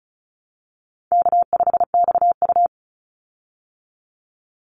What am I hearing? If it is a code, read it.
Morse code, heard as a keyed tone, about 34 wpm, 700 Hz. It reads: K5XU
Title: Call Signs Repeated 3x - 35wpm